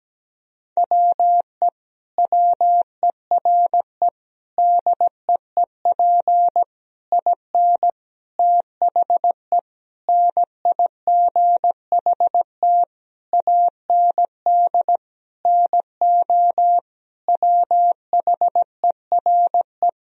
Morse code, heard as a keyed tone, about 17 wpm, 700 Hz. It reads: WE WERE DEEP IN THE NIGHT AND NO WHERE
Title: Sentences from Top 1000 Words - Review Part 3 - 17wpm